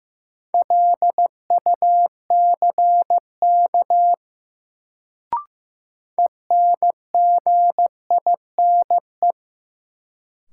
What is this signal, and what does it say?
Morse code, heard as a keyed tone, about 15 wpm, 700 Hz. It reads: LUCK ENGINE
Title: Sentences from Top 1500 Words - New Word Drill - 15wpm